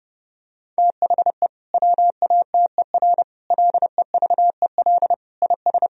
Morse code, heard as a keyed tone, about 30 wpm, 700 Hz. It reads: THE WATER LEVEL IS
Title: Sentences from Top 1000 Words - Review Part 4 - 30wpm